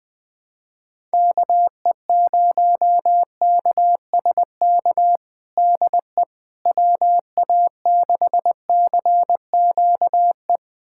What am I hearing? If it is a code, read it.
Morse code, heard as a keyed tone, about 21 wpm, 700 Hz. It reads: KE0KSK DE WA6CQE
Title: USA Callsign DE Callsign 20wpm